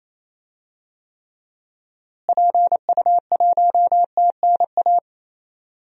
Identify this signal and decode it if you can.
Morse code, heard as a keyed tone, about 28 wpm, 700 Hz. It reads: PU1TNA
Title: DXCC-100 Call Signs - 6 Characters - Repeated 2x - 28wpm